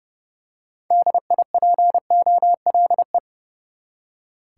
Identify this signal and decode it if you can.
Morse code, heard as a keyed tone, about 30 wpm, 700 Hz. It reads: DIPOLE
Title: Top 90 QSO Elements 30wpm